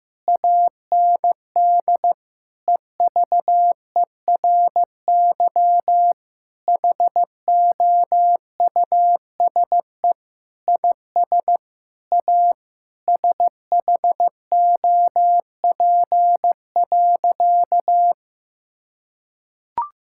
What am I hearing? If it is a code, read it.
Morse code, heard as a keyed tone, about 15 wpm, 700 Hz. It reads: AND EVERY HOUSE IS A SHOP.
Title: Sentences from Top 1600 Words 15wpm